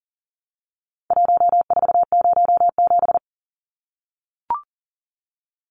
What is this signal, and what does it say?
Morse code, heard as a keyed tone, about 40 wpm, 700 Hz. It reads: 1407
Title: Numbers - Contest Style - 4-Digit 40wpm